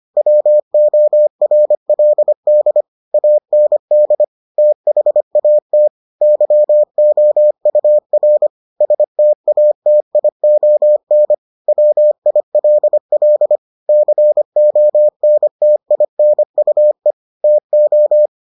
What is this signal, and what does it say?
Morse code, heard as a keyed tone, about 25 wpm, 600 Hz. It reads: WORLD AND THAT YOUR STATION WILL CONTINUE TO